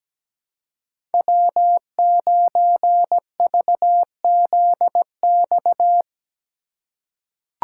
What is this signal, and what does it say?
Morse code, heard as a keyed tone, about 17 wpm, 700 Hz. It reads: W9VZX
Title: USA Call Signs - 3x Character Spacing - 17wpm